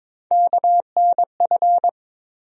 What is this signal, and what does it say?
Morse code, heard as a keyed tone, about 22 wpm, 700 Hz. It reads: KNF
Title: USA Call Sign Postfixes - 3 Letters 22wpm